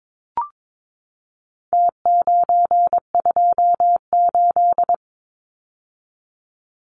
Morse code, heard as a keyed tone, about 22 wpm, 700 Hz. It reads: T928
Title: Numbers - Contest Style - 4-Digit 22wpm